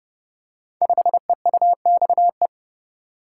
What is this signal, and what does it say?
Morse code, heard as a keyed tone, about 30 wpm, 700 Hz. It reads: 5EUXE
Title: Code Groups - 5 Letter:Numbers - 30wpm